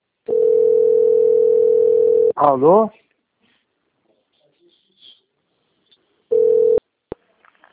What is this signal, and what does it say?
Ringback tone (North America), 440 + 480 Hz, with speech.
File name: rg-1600-unknown- - - .64951